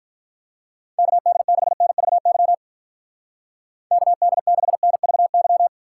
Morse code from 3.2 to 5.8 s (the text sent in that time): KD6NVY